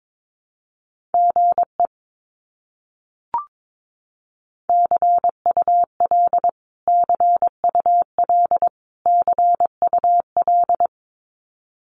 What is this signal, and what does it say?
Morse code, heard as a keyed tone, about 22 wpm, 700 Hz. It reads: GE CUL CUL CUL
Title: Abbreviations Repeated 3x - 22wpm